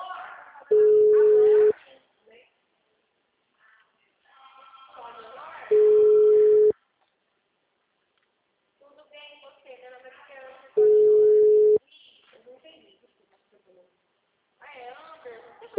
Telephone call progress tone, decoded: ringback tone

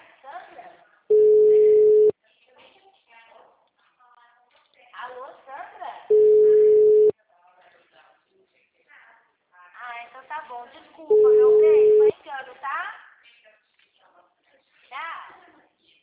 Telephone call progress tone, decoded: ringback tone

